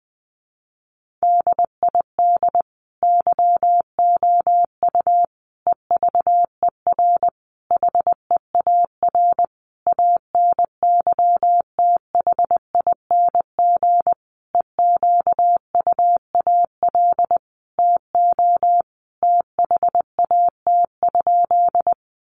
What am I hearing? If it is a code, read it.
Morse code, heard as a keyed tone, about 20 wpm, 700 Hz. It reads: DID YOU EVER HEAR ANYTHING EQUAL TO THAT?
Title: Sentences from Top 1400 Words 20wpm